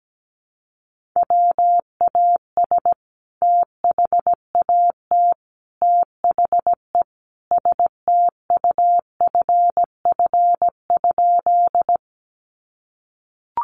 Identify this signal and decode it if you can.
Morse code, heard as a keyed tone, about 17 wpm, 700 Hz. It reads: WAS THAT THE STUFF?
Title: Sentences from Top 1400 Words 17wpm